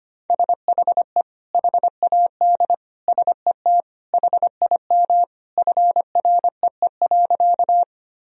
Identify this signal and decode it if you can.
Morse code, heard as a keyed tone, about 25 wpm, 700 Hz. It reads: SHE HAD SET HIM FREE.
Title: Sentences from Top 1000 Words - Review Part 1 - 25wpm